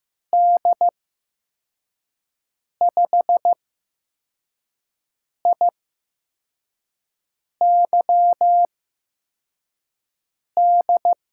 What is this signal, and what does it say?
Morse code, heard as a keyed tone, about 15 wpm, spaced as for 4 wpm, 700 Hz. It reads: D5IYD
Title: USA Call Signs - 6 Characters - 8x Character Spacing - 15wpm